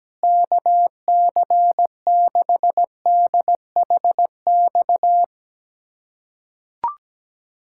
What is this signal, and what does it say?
Morse code, heard as a keyed tone, about 17 wpm, 700 Hz. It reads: KC6DHX
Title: USA Callsign DE Callsign 17wpm